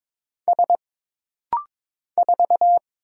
Morse code, heard as a keyed tone, about 22 wpm, 700 Hz. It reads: S 4
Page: This rapid-fire timing is geared to push your ICR (Instant Character Recognition) skills.